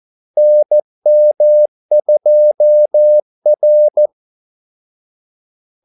Morse code, heard as a keyed tone, about 14 wpm, 600 Hz. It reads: NM2R
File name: Callsigns-14wpm